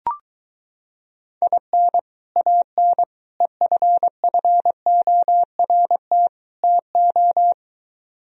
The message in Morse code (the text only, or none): IN AN EFFORT TO